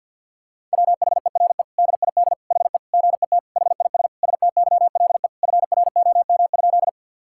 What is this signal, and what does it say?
Morse code, heard as a keyed tone, about 50 wpm, 700 Hz. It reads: WHERE DID HE GET HIS STYLE FROM?
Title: Sentences from Top 1000 Words - Review Part 5 - 50wpm